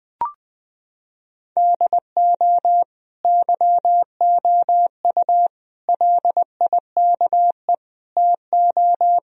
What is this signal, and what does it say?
Morse code, heard as a keyed tone, about 20 wpm, 700 Hz. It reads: DO YOU LIKE TO
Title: Sentences from Top 1000 Words - Review Part 1 - 20wpm